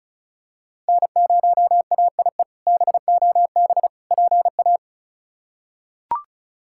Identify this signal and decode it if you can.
Morse code, heard as a keyed tone, about 36 wpm, 700 Hz. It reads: N0AIE BOB PA
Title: K1USN SST QSO Practice - 35wpm